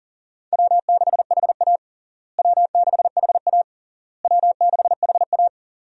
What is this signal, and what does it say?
Morse code, heard as a keyed tone, about 40 wpm, 700 Hz. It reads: W6HA W6HA W6HA
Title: USA Call Signs - 4 Characters - Repeated 3x - 40wpm